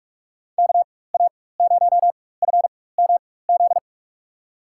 Morse code, heard as a keyed tone, about 44 wpm, spaced as for 21 wpm, 700 Hz. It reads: KA0FMZ
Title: USA Call Signs - 6 Characters - 4x Character Spacing - 45wpm